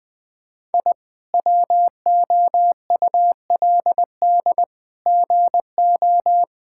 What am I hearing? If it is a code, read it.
Morse code, heard as a keyed tone, about 21 wpm, 700 Hz. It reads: I WOULD GO
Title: Sets of 8 Words - 20wpm